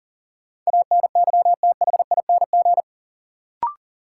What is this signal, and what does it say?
Morse code, heard as a keyed tone, about 40 wpm, 700 Hz. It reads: ANYTHING
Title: Sentences from Top 1000 Words - New Word Drill - 40wpm